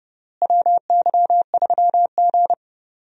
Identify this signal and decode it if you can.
Morse code, heard as a keyed tone, about 30 wpm, 700 Hz. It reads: WY3G
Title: USA Call Signs - 4 Characters - Repeated 3x - 30wpm